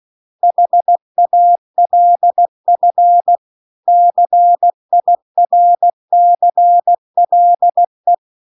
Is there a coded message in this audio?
Yes: Morse code from 0.4 to 8.1 s, heard as a keyed tone, about 16 wpm, 700 Hz. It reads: HALF CIRCLE